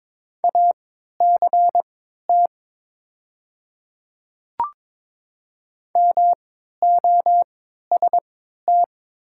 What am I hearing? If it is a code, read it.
Morse code, heard as a keyed tone, about 22 wpm, spaced as for 12 wpm, 700 Hz. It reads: ACT MOST
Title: 3x Character Spacing - Top 300 Words - 22wpm